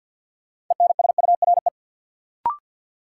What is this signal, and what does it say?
Morse code, heard as a keyed tone, about 52 wpm, 700 Hz. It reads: ENSURE
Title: Sentences from Top 1100 Words - New Word Drill Alt - 50wpm